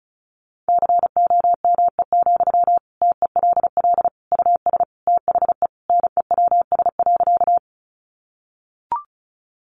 Morse code, heard as a keyed tone, about 34 wpm, 700 Hz. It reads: COME, TELL US THE NEWS.
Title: Sentences from Top 800 Words 35wpm